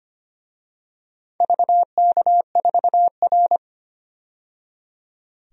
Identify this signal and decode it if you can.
Morse code, heard as a keyed tone, about 25 wpm, 700 Hz. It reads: VK4R
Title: DXCC-100 Call Signs - 4 Characters - 25wpm